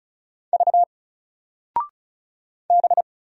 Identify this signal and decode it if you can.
Morse code, heard as a keyed tone, about 36 wpm, 700 Hz. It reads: V B
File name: Single-Letter-Number-35wpm